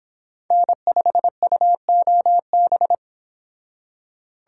Morse code, heard as a keyed tone, about 28 wpm, 700 Hz. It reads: N5UOB